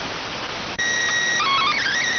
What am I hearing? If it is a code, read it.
SSTV (slow-scan television), PD 290.